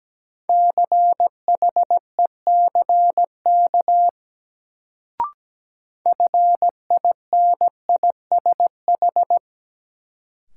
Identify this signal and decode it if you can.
Morse code, heard as a keyed tone, about 17 wpm, 700 Hz. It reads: CHECK FINISH
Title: Sentences from Top 1000 Words - Review Part 3 - New Word Drill - 17wpm